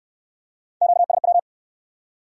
Morse code, heard as a keyed tone, about 68 wpm, 700 Hz. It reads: QSW